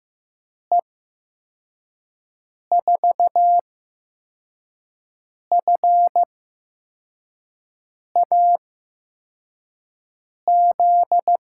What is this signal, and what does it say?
Morse code, heard as a keyed tone, about 15 wpm, spaced as for 4 wpm, 700 Hz. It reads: E4FAZ
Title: USA Call Signs - 8x Character Spacing - 15wpm